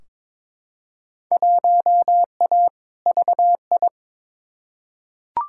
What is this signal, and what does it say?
Morse code, heard as a keyed tone, about 22 wpm, 700 Hz. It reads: 1A VI
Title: Field Day Exchanges Encore - 22wpm